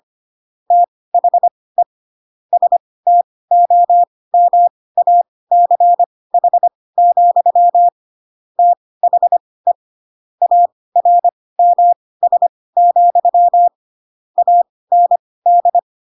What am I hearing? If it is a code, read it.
Morse code, heard as a keyed tone, about 25 wpm, spaced as for 18 wpm, 700 Hz. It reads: THE STOMACH, THE ARMS, AND
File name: Pinocchio ch3 12 25 ik1hge